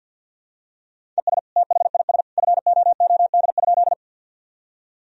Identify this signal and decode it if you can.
Morse code, heard as a keyed tone, about 52 wpm, 700 Hz. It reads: ES THIS FOOD?
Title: Sentences from Top 300 Words and Abbreviations 50wpm